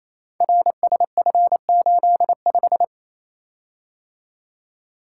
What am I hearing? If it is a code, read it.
Morse code, heard as a keyed tone, about 27 wpm, 700 Hz. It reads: RSF85
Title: Code Groups - 5 Letter:Numbers - 28wpm